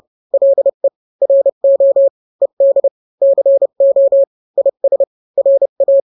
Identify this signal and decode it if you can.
Morse code, heard as a keyed tone, about 30 wpm, 550 Hz. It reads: LE RO ED CO IS RA
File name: n-grams / 2-grams 30 25 WPM